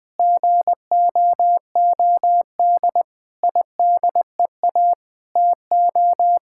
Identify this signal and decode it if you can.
Morse code, heard as a keyed tone, about 20 wpm, 700 Hz. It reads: GOOD IDEA TO
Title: Sentences with 9-Word Phrases - Spoken Beginning - 20wpm